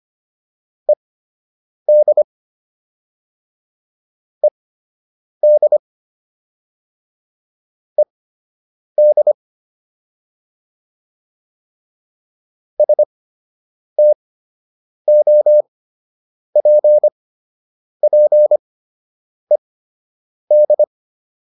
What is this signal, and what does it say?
Morse code, heard as a keyed tone, about 25 wpm, spaced as for 8 wpm, 600 Hz. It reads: ED ED ED STOPPED